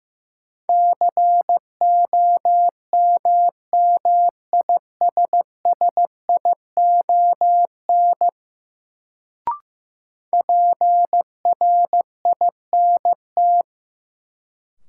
Morse code, heard as a keyed tone, about 15 wpm, 700 Hz. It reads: COMMISSION PRINT
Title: Sentences from Top 1400 Words - New Word Drill - 15wpm